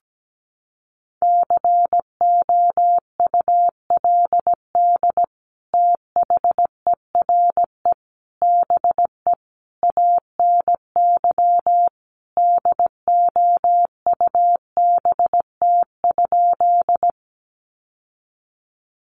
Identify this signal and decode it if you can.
Morse code, heard as a keyed tone, about 17 wpm, 700 Hz. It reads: COULD THERE BE ANY DOUBT?
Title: Sentences from Top 1500 Words 17wpm